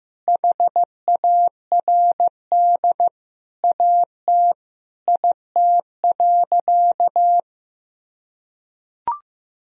Morse code, heard as a keyed tone, about 16 wpm, 700 Hz. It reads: HARD AT IT.
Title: Sentences from Top 300 Words and Abbreviations 15wpm